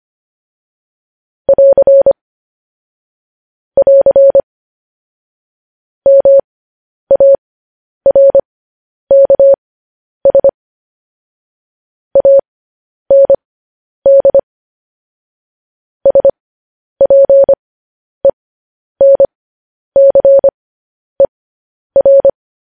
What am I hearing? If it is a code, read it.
Morse code, heard as a keyed tone, about 25 wpm, spaced as for 10 wpm, 550 Hz. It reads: <AR> <AR> MARKS AND SPENCER
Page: news-headlines-in-morse-code-at-10-wpm